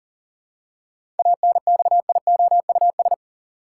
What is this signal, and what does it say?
Morse code, heard as a keyed tone, about 40 wpm, 700 Hz. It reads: ANXIOUS
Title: Seven Letter Words - 40wpm